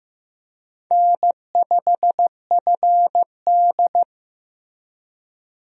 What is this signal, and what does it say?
Morse code, heard as a keyed tone, about 15 wpm, 700 Hz. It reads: N5FD
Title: USA Call Signs - 4 Characters - Repeated 2x - 15wpm